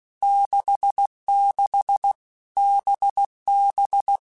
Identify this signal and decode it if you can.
Morse code, heard as a keyed tone, about 16 wpm, 800 Hz. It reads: 66 BB
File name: Basic Letter Groups By7Groups / VB46 / VB46@15